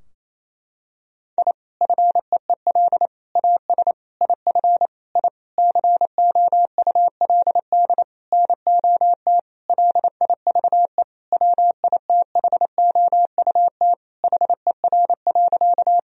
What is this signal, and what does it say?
Morse code, heard as a keyed tone, about 28 wpm, 700 Hz. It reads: I FEEL AS IF I COULD NOT LIVE WITHOUT HER.
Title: Sentences from Top 1000 Words - Review Part 3 - 28wpm